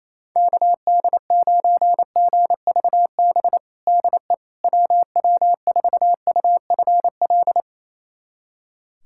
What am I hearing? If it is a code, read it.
Morse code, heard as a keyed tone, about 28 wpm, 700 Hz. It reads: KD9GVB DE WW4UFL
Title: USA Callsign DE Callsign - 6 Characters - 28wpm